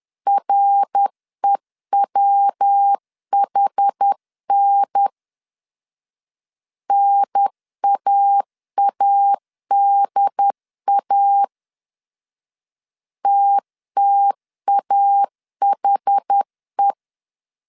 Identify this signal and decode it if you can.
Morse code, heard as a keyed tone, about 10 wpm, 800 Hz. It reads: REWHN NAADA TTAHE
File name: 10wpm